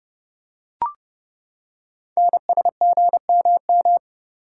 Morse code, heard as a keyed tone, about 30 wpm, 700 Hz. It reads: NSGMM
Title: Code Groups - 5 Letters - 30wpm